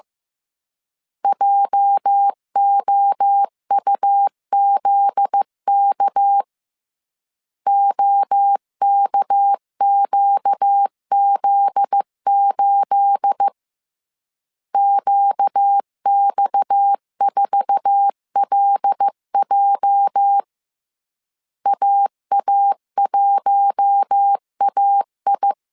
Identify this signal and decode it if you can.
Morse code, heard as a keyed tone, about 15 wpm, 800 Hz. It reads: JOUZK OKQZ8 QX4LJ AA1AI
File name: CW / DET / 1 9 1500C S C 14wpm 3245